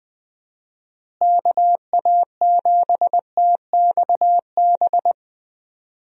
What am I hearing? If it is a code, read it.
Morse code, heard as a keyed tone, about 20 wpm, 700 Hz. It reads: KA7TXB
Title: USA Call Signs - 6 Characters - Repeated 2x - 20wpm